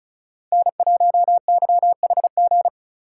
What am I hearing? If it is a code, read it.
Morse code, heard as a keyed tone, about 34 wpm, 700 Hz. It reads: N1YHG
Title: USA Call Signs - 5 Characters - 4x Character Spacing - 35wpm